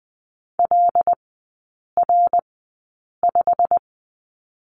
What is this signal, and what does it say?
Morse code, heard as a keyed tone, about 20 wpm, spaced as for 8 wpm, 700 Hz. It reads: LR5
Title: 087 Lesson 15e - Number 5 - Three Characters 20wpm